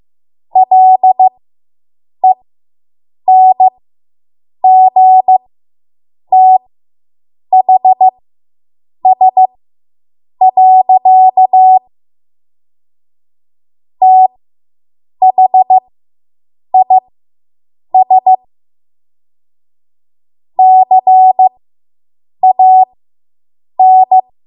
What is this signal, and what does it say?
Morse code, heard as a keyed tone, about 15 wpm, spaced as for 7 wpm, 750 Hz. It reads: LENGTHS. THIS CAN